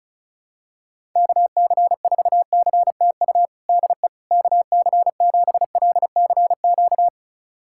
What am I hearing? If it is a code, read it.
Morse code, heard as a keyed tone, about 36 wpm, 700 Hz. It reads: KC4CTU DE KC7LCQ
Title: USA Callsign DE Callsign - 6 Characters - 35wpm